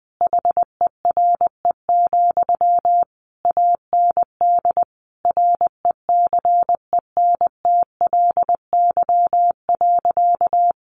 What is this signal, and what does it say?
Morse code, heard as a keyed tone, about 20 wpm, 700 Hz. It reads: HERE, AND RECENTLY.
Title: Sentences from Top 1200 Words 20wpm